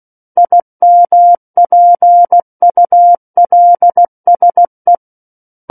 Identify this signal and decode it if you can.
Morse code, heard as a keyed tone, about 16 wpm, 700 Hz. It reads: IMPULSE
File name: Book One / 07-Chapter 7 - 16 WPM